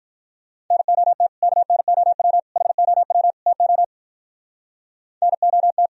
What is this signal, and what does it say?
Morse code, heard as a keyed tone, about 46 wpm, 700 Hz. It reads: NOT KNOW HOW TO NOT